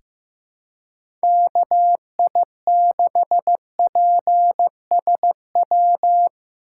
Morse code, heard as a keyed tone, about 15 wpm, 700 Hz. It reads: KI6PSW
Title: USA Call Signs - 3x Character Spacing - 15wpm